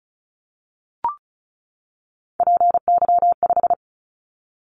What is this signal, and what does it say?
Morse code, heard as a keyed tone, about 36 wpm, 700 Hz. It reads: PY5
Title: Callsign Prefixes for Top 100 DXCC - 35wpm